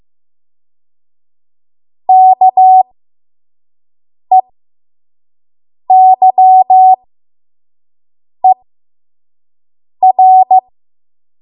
Morse code, heard as a keyed tone, about 15 wpm, spaced as for 5 wpm, 750 Hz. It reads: KEYER
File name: Morse / 05WPM